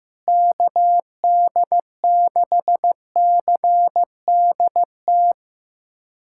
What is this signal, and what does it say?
Morse code, heard as a keyed tone, about 15 wpm, 700 Hz. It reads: KD6CDT
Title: USA Callsign DE Callsign 15wpm